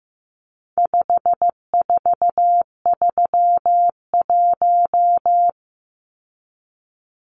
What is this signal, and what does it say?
Morse code, heard as a keyed tone, about 15 wpm, 700 Hz. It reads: 5431
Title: Numbers - 4-Digit 15wpm